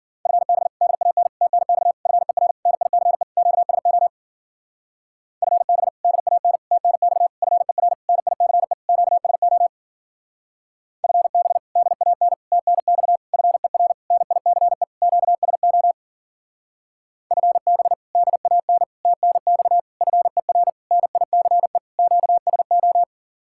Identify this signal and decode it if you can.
Morse code, heard as a keyed tone, about 48 wpm, 700 Hz. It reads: FB DAN TNX FER NICE QSO FB DAN TNX FER NICE QSO FB DAN TNX FER NICE QSO FB DAN TNX FER NICE QSO
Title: Speed-Racing Common QSO Phrases - 40wpm